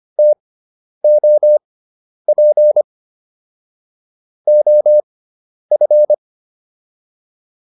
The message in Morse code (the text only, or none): TOP OF